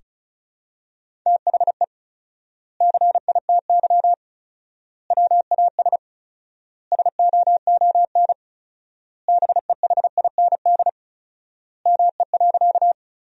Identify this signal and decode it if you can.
Morse code, heard as a keyed tone, about 36 wpm, 700 Hz. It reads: THE CITY WAS SOON BEHIND ME.
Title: 4x Word-Spacing - Sentences from Top 500 Words - 35wpm